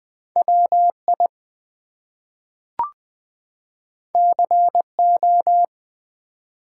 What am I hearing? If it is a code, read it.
Morse code, heard as a keyed tone, about 20 wpm, 700 Hz. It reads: WI CO
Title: US States and Canadian Provinces 20wpm